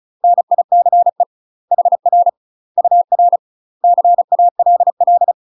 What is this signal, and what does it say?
Morse code, heard as a keyed tone, about 34 wpm, 700 Hz. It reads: NICE HR UR CALL
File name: UK MASTER QSOs TRAINING/35WPM